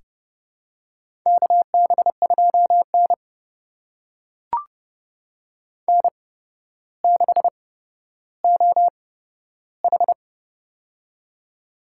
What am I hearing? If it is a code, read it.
Morse code, heard as a keyed tone, about 31 wpm, spaced as for 13 wpm, 700 Hz. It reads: KB2N N6OH
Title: USA Call Signs - 4 Characters - 8x Character Spacing - 30wpm